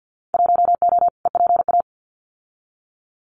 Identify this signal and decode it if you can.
Morse code, heard as a keyed tone, about 50 wpm, 700 Hz. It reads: 1O EPA
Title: Winter Field Day Exchanges 50wpm